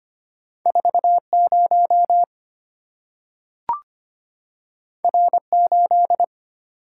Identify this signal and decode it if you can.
Morse code, heard as a keyed tone, about 25 wpm, 700 Hz. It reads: 40 R8
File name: Two Letter-Number Combinations - 25wpm